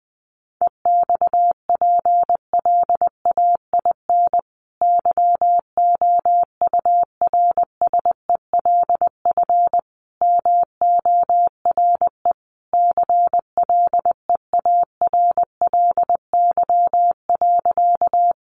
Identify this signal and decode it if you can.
Morse code, heard as a keyed tone, about 20 wpm, 700 Hz. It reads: EXPLAIN YOURSELF MORE CLEARLY.
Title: Sentences from Top 1300 Words 20wpm